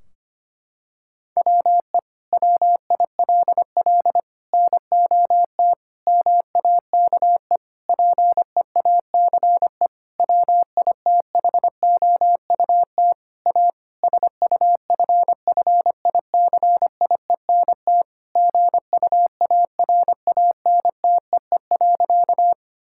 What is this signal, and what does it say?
Morse code, heard as a keyed tone, about 25 wpm, 700 Hz. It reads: WE WILL NOT MAKE PEACE WITHOUT A SUFFICIENT GUARANTEE.
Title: Sentences from Top 1600 Words 25wpm